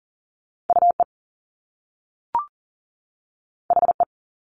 Morse code, heard as a keyed tone, about 38 wpm, 700 Hz. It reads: UE HE
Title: Top Two Letter Combinations in Words 40wpm